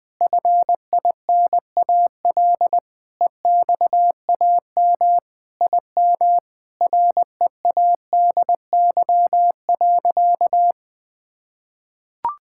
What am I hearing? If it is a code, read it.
Morse code, heard as a keyed tone, about 20 wpm, 700 Hz. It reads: FINAL EXAM IM READY.
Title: Sentences with 5-Word Phrases - 20wpm